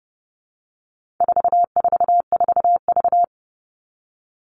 Morse code, heard as a keyed tone, about 30 wpm, 700 Hz. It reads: 444V
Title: All Cut Numbers - 4-Digit 30wpm